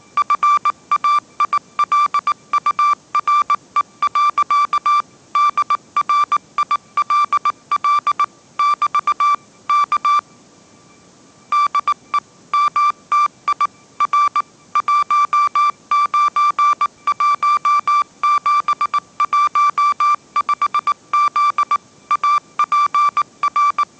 Morse code, heard as a keyed tone, about 19 wpm, 1170 Hz. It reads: FAILURE. DRILL <BT> K DE MTI R 191715Z APR
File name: mrx083 20wpm